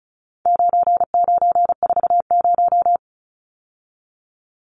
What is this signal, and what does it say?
Morse code, heard as a keyed tone, about 34 wpm, 700 Hz. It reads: 9940
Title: Numbers - 4-Digit 35wpm